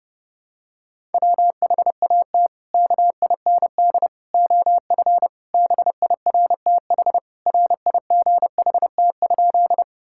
Morse code, heard as a keyed tone, about 30 wpm, 700 Hz. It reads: WHAT KIND OF BIRTH RIGHT?
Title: Sentences from Top 1000 Words - Review Part 5 - 30wpm